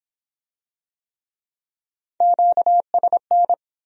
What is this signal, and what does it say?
Morse code, heard as a keyed tone, about 25 wpm, 700 Hz. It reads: QSN